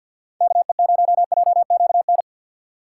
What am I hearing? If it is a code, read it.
Morse code, heard as a keyed tone, about 50 wpm, 700 Hz. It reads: KE0JQN
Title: USA Call Signs - 6 Characters - 3x Character Spacing - 50wpm